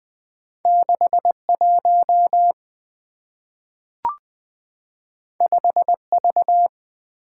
Morse code, heard as a keyed tone, about 20 wpm, 690 Hz. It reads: 61 5V
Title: Two Letter-Number Combinations - 20wpm